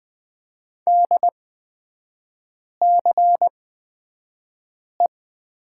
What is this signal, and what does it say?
Morse code, heard as a keyed tone, about 20 wpm, spaced as for 5 wpm, 700 Hz. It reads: DCE
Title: 224 Lesson 36e - BT - Three Characters 20wpm